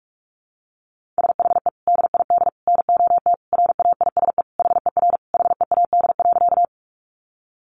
Morse code, heard as a keyed tone, about 44 wpm, 700 Hz. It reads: SHE DID NOT RAISE HER HEAD.
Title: Sentences from Top 1400 Words 45wpm